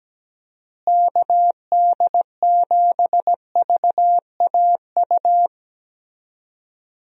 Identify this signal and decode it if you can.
Morse code, heard as a keyed tone, about 17 wpm, 700 Hz. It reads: KD7VAU
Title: USA Call Signs - 6 Characters - Repeated 2x - 17wpm